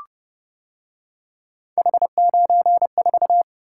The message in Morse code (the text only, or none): H94